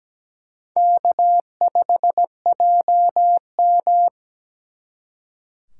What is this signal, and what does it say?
Morse code, heard as a keyed tone, about 20 wpm, 700 Hz. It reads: K5JM
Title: Speed Racing - USA 1x2 Call Signs 17wpm